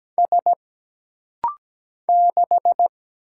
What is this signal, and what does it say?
Morse code, heard as a keyed tone, about 17 wpm, 700 Hz. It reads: S 6
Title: Single Letter-Number - Rapid-Fire 17wpm